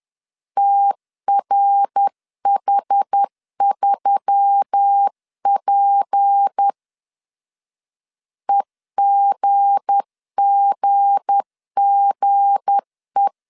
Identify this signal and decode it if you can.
Morse code, heard as a keyed tone, about 10 wpm, 800 Hz. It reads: TRH3P EGGGE
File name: LEZIONI / 7 GF9Z / 7 G 10wpm 2501